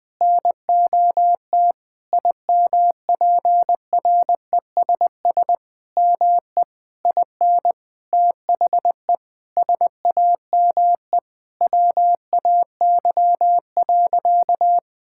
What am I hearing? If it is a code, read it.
Morse code, heard as a keyed tone, about 21 wpm, 700 Hz. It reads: NOT IMPRESS ME IN THE SAME WAY.
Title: Sentences from Top 1600 Words 20wpm